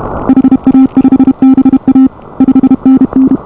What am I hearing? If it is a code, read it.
Morse code, heard as a keyed tone, about 29 wpm, 280 Hz. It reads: HA5BA 5NN